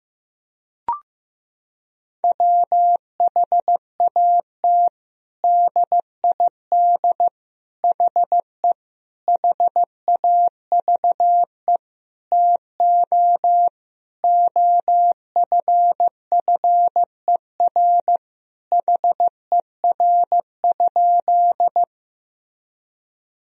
Morse code, heard as a keyed tone, about 16 wpm, 700 Hz. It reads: WHAT DID HE HAVE TO OFFER HER?
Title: Sentences from Top 1000 Words - Review Part 4 - 15wpm